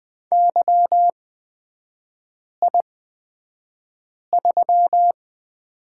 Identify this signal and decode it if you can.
Morse code, heard as a keyed tone, about 20 wpm, spaced as for 5 wpm, 690 Hz. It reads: YI3
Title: 236 Lesson 38e - Letter q - Three Characters 20wpm